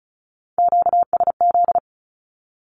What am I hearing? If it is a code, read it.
Morse code, heard as a keyed tone, about 34 wpm, 700 Hz. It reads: QSZ
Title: Q-Codes 35wpm